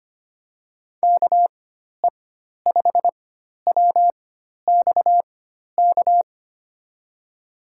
Morse code, heard as a keyed tone, about 25 wpm, spaced as for 12 wpm, 700 Hz. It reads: KE5WXK
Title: USA Call Signs - 4x Character Spacing - 25wpm